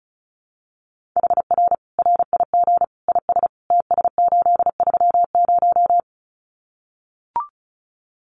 Morse code, heard as a keyed tone, about 34 wpm, 700 Hz. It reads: HR RIG IS TS830
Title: Common QSO Phrases 35wpm